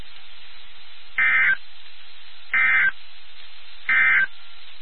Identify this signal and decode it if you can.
Emergency Alert System (SAME) tones, heard as harsh buzzing bursts. It is heard from 1.1 to 4.4 s.
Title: NOAA Weather Radio WXK79